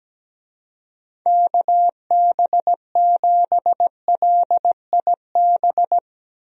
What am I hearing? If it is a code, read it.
Morse code, heard as a keyed tone, about 17 wpm, 700 Hz. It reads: KB7LIB
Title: USA Call Signs - 6 Characters - Repeated 2x - 17wpm